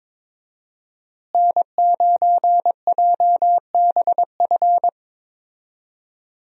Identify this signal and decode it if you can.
Morse code, heard as a keyed tone, about 22 wpm, 700 Hz. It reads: N9JBF
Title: USA Call Signs - 5 Characters - Repeated 2x - 22wpm